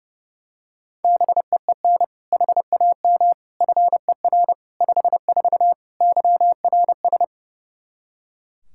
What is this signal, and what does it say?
Morse code, heard as a keyed tone, about 30 wpm, 700 Hz. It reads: BEEN HAM FER 54 YRS
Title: K1USN SST QSO Practice 30wpm